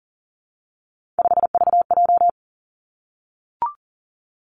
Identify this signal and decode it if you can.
Morse code, heard as a keyed tone, about 40 wpm, 700 Hz. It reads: 5VJ
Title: Three Challenging Letter-Number Combinations 40wpm